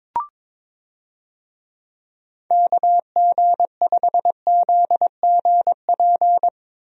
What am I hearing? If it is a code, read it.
Morse code, heard as a keyed tone, about 22 wpm, 700 Hz. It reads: KG5ZGP
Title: USA Call Signs - 6 Characters - Repeated 3x - 22wpm